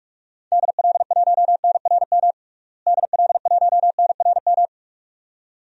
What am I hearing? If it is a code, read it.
Morse code, heard as a keyed tone, about 46 wpm, 700 Hz. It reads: DL1NRM DL1NRM
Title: DXCC-100 Call Signs - 6 Characters - Repeated 2x - 45wpm